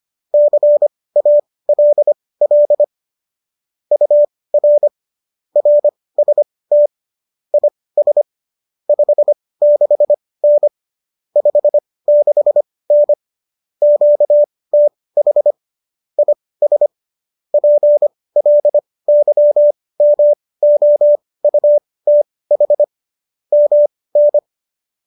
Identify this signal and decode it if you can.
Morse code, heard as a keyed tone, about 25 wpm, spaced as for 18 wpm, 590 Hz. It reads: CALL UR RST IS 56N 56N QTH IS PLYMOUTH MN